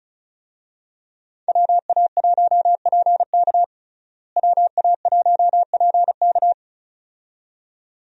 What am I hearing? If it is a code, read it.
Morse code, heard as a keyed tone, about 34 wpm, 700 Hz. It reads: WA1PK WA1PK
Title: USA Call Signs - 5 Characters - Repeated 2x - 35wpm